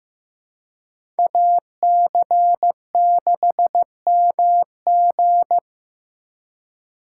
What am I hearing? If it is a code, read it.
Morse code, heard as a keyed tone, about 16 wpm, 700 Hz. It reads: AC6MG
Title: USA Call Signs - 5 Characters - 2x Character Spacing - 15wpm